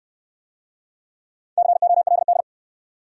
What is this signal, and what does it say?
Morse code, heard as a keyed tone, about 64 wpm, 700 Hz. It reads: COLD